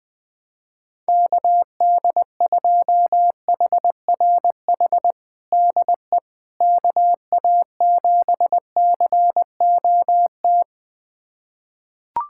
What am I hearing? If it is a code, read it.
Morse code, heard as a keyed tone, about 20 wpm, 700 Hz. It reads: KD2HRH DE KA7COT
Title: USA Callsign DE Callsign 20wpm